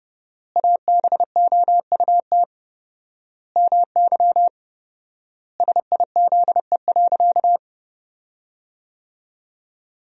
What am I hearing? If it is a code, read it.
Morse code, heard as a keyed tone, about 30 wpm, 700 Hz. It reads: ABOUT MY SIZE.
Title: 4x Word-Spacing - Sentences from Top 400 Words - 30wpm